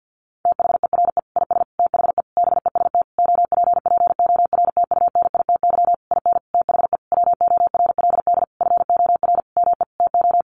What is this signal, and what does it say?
Morse code, heard as a keyed tone, about 50 wpm, 700 Hz. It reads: THERE IS THE BEST OPPORTUNITY IN THE WORLD FOR ME TO
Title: Sentences from Top 1500 Words 50wpm